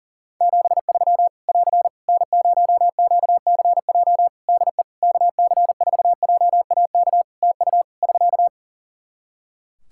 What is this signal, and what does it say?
Morse code, heard as a keyed tone, about 40 wpm, 700 Hz. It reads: 73 <AR> N0QCJ DE KC4JAK TU <SK>
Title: K1USN SST QSO Practice 40wpm